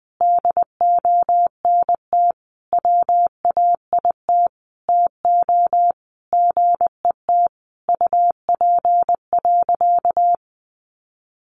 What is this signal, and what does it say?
Morse code, heard as a keyed tone, about 20 wpm, 700 Hz. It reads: DONT WAIT TO GET UP.
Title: Sentences from Top 500 Words 20wpm